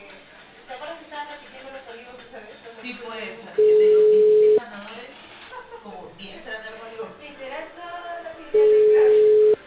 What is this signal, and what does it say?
Ringback tone (Europe, 425 Hz), with speech.